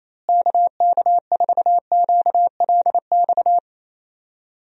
Morse code, heard as a keyed tone, about 28 wpm, 700 Hz. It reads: KK4QLX
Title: USA Call Signs - 6 Characters - Repeated 3x - 28wpm